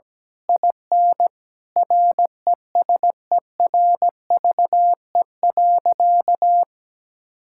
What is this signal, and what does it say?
Morse code, heard as a keyed tone, about 17 wpm, 700 Hz. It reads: IN RESERVE.
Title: Sentences from Top 1000 Words - Review Part 6 - 17wpm